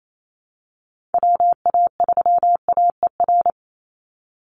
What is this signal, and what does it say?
Morse code, heard as a keyed tone, about 28 wpm, 700 Hz. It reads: WA3AER
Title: Call Signs Repeated 2x - 28wpm